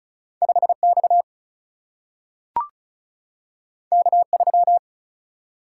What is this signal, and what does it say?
Morse code, heard as a keyed tone, about 36 wpm, 700 Hz. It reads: 5X K3
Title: Single-Letter Single-Number 35wpm